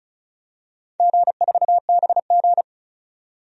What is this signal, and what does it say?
Morse code, heard as a keyed tone, about 36 wpm, 700 Hz. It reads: G4BG
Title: DXCC-100 Call Signs - 4 Characters - 35wpm